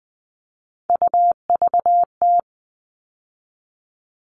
Morse code, heard as a keyed tone, about 20 wpm, 700 Hz. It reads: UVT
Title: All Cut Numbers - 3-Digit 20wpm